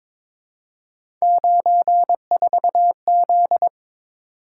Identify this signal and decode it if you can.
Morse code, heard as a keyed tone, about 22 wpm, 700 Hz. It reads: 94Z
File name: practice / 22wpm / Three Letter-Number Combinations - 22wpm